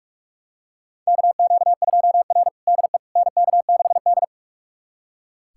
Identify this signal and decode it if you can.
Morse code, heard as a keyed tone, about 43 wpm, 700 Hz. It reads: KQ2R DE NK6D
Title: USA Callsign DE Callsign - 4 Characters - 45wpm